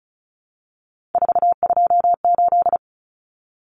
Morse code, heard as a keyed tone, about 34 wpm, 700 Hz. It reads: 428
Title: Common Cut Numbers - 3-Digit 35wpm